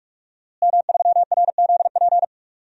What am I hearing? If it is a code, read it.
Morse code, heard as a keyed tone, about 46 wpm, 700 Hz. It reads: M3RZP
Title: DXCC-100 Call Signs - 5 Characters - Repeated 2x - 45wpm